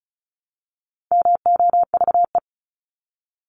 Morse code, heard as a keyed tone, about 34 wpm, 700 Hz. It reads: MOVE
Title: ING Suffix - Top 100 Words with Base Words 35wpm